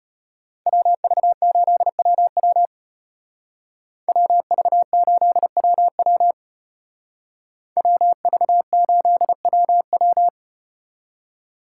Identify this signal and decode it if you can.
Morse code, heard as a keyed tone, about 32 wpm, 700 Hz. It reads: WV8WW WV8WW WV8WW